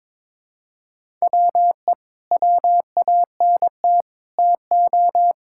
Morse code, heard as a keyed tone, about 22 wpm, 700 Hz. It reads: WE WANT TO
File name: practice / 22wpm / Sentences with 3-Word Phrases - 22wpm